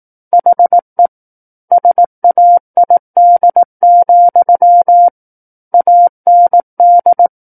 Morse code, heard as a keyed tone, about 18 wpm, 700 Hz. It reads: HE SAID, AND
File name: Book One / 09-Chapter 9 - 18 WPM